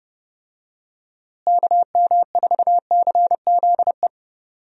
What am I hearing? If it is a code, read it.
Morse code, heard as a keyed tone, about 30 wpm, 700 Hz. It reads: KM4CZE
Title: USA Call Signs - 6 Characters - Repeated 3x - 30wpm